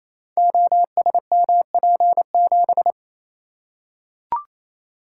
Morse code, heard as a keyed tone, about 28 wpm, 700 Hz. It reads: OSMP7
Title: Code Groups - 5 Letter:Numbers - 28wpm